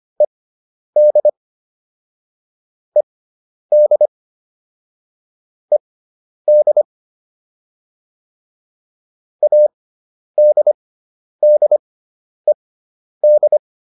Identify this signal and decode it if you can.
Morse code, heard as a keyed tone, about 25 wpm, spaced as for 10 wpm, 600 Hz. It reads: ED ED ED ADDED
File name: ED5 10